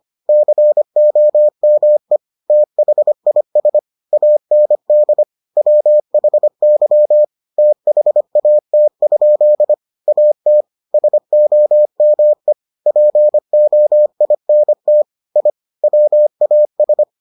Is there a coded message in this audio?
Yes: Morse code from 0.3 to 17.0 s, heard as a keyed tone, about 25 wpm, 600 Hz. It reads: COME THIS AND WHY THAT? AT SOME POINT I WAS